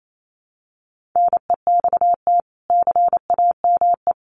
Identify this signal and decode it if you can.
Morse code, heard as a keyed tone, about 28 wpm, 700 Hz. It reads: NEXT CAME
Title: Sentences from Top 1500 Words 28wpm